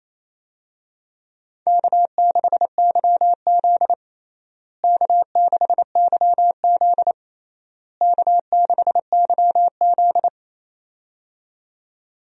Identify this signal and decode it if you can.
Morse code, heard as a keyed tone, about 28 wpm, 700 Hz. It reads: K6YZ K6YZ K6YZ
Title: USA Call Signs - 4 Characters - Repeated 3x - 28wpm